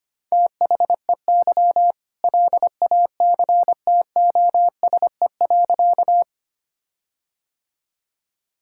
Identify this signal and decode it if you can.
Morse code, heard as a keyed tone, about 25 wpm, 700 Hz. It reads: THEY LACTOSE.
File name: practice / 25wpm / All-Age Appropriate Jokes - 25wpm